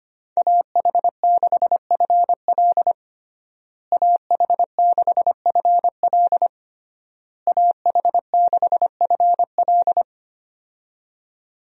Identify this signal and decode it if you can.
Morse code, heard as a keyed tone, about 25 wpm, 700 Hz. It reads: AH6FL AH6FL AH6FL
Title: USA Call Signs - 5 Characters - Repeated 3x - 25wpm